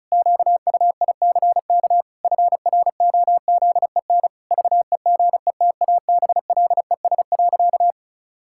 Morse code, heard as a keyed tone, about 34 wpm, 700 Hz. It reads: QUICK FROZEN VEGETABLES.
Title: Pangram Challenge v2 - 35wpm